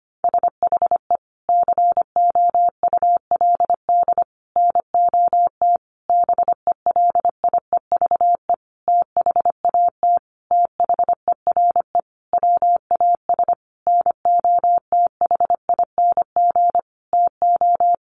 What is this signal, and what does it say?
Morse code, heard as a keyed tone, about 25 wpm, 700 Hz. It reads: SHE COULD NOT BELIEVE THAT THERE WAS NOTHING TO
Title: Sentences from Top 1000 Words 25wpm